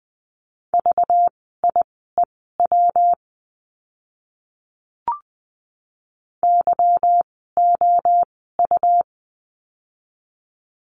Morse code, heard as a keyed tone, about 20 wpm, spaced as for 14 wpm, 700 Hz. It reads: VIEW YOU
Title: 192 Lesson 31f - Letter v - Words 20wpm